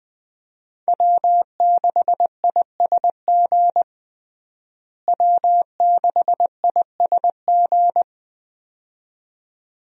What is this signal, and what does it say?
Morse code, heard as a keyed tone, about 21 wpm, 700 Hz. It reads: W6ISG W6ISG
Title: USA Call Signs - 5 Characters - Repeated 2x - 20wpm